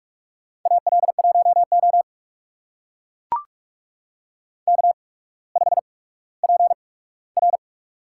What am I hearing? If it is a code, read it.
Morse code, heard as a keyed tone, about 44 wpm, spaced as for 19 wpm, 700 Hz. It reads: AL1O K5PR
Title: USA Call Signs - 4 Characters - 8x Character Spacing - 45wpm